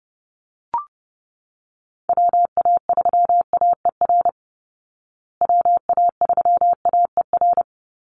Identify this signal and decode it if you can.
Morse code, heard as a keyed tone, about 30 wpm, 700 Hz. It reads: WA3AER WA3AER
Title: Call Signs Repeated 3x - 30wpm